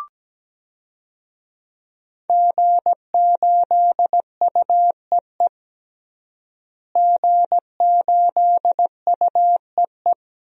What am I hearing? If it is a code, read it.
Morse code, heard as a keyed tone, about 17 wpm, 700 Hz. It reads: G8UEE G8UEE
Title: DXCC-100 Call Signs - 5 Characters - Repeated 3x - 17wpm